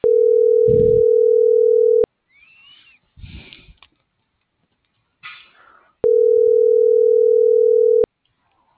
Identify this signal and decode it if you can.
Ringback tone (North America), 440 + 480 Hz.